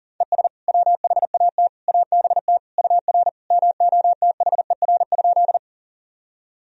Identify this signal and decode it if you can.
Morse code, heard as a keyed tone, about 37 wpm, 700 Hz. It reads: ES WHAT ABT UR MOTHER?
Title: Sentences from Top 200 Words and Abbreviations 40wpm